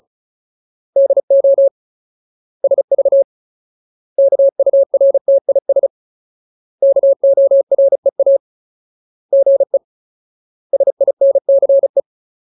Morse code, heard as a keyed tone, about 34 wpm, 550 Hz. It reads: DO SV KURTIS KOREA GE SINCE
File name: Ultimate Word List / 35WPM Ultimate